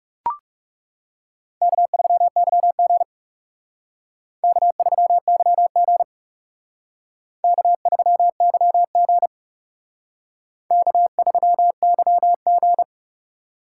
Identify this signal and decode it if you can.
Morse code, heard as a keyed tone, about 38 wpm, 700 Hz. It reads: K3YG K3YG K3YG K3YG
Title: Speed Racing - USA 1x2 Call Signs 30wpm